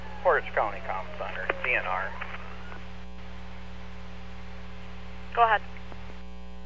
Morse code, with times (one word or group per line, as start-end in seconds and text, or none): none